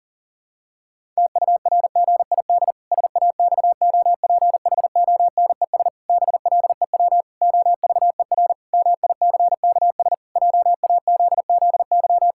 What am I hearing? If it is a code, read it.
Morse code, heard as a keyed tone, about 40 wpm, 700 Hz. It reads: TURGID SAXOPHONES BLEW OVER MICKS JAZZY
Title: Pangram Challenge v2 - 40wpm